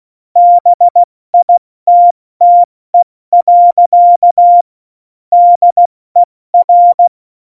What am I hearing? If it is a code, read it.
Morse code, heard as a keyed tone, about 16 wpm, 700 Hz. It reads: BITTE. DER
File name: Morsekurs / Witze / Witz 15. Oktober 2019, 70 F80